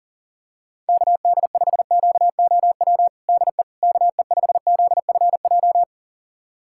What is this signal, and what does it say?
Morse code, heard as a keyed tone, about 40 wpm, 700 Hz. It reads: KD5QOW DE KE5ZFJ
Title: USA Callsign DE Callsign 40wpm